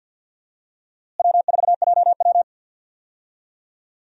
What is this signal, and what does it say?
Morse code, heard as a keyed tone, about 53 wpm, 700 Hz. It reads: W4JW
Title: USA Call Signs - 4 Characters - Repeated 3x - 50wpm